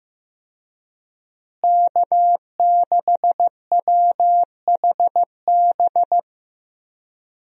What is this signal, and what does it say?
Morse code, heard as a keyed tone, about 15 wpm, 700 Hz. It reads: K6WHB
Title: USA Call Signs - 5 Characters - Repeated 2x - 15wpm